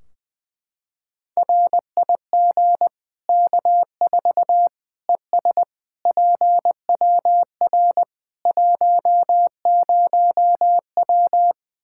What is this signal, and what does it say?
Morse code, heard as a keyed tone, about 20 wpm, 700 Hz. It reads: RIG K4 ES PWR 10W
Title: Practice QSO - Exchange 2 - Info - 20wpm